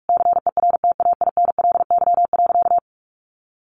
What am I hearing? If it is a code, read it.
Morse code, heard as a keyed tone, about 44 wpm, 700 Hz. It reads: CERTAINLY.
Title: Sentences from Top 900 Words 45wpm